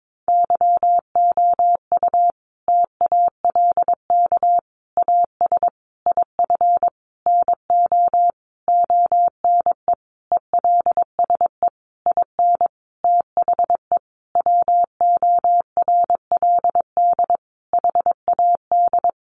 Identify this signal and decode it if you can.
Morse code, heard as a keyed tone, about 22 wpm, 700 Hz. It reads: YOU TALK AS IF NO ONE ELSE IN THE WORLD HAD
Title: Sentences from Top 1100 Words 22wpm